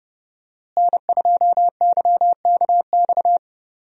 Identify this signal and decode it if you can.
Morse code, heard as a keyed tone, about 30 wpm, 700 Hz. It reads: N2YKX
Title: USA Call Signs - 5 Characters - Repeated 2x - 30wpm